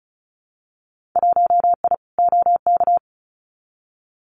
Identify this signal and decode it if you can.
Morse code, heard as a keyed tone, about 36 wpm, 700 Hz. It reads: 1I OK
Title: Winter Field Day Exchanges Encore 35wpm